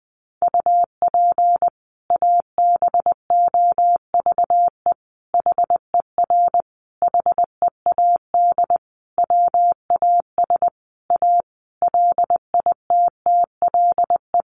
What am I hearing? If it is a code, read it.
Morse code, heard as a keyed tone, about 20 wpm, 700 Hz. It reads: UP ABOVE HER HEAD WAS A LITTLE
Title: Sentences from Top 1200 Words 20wpm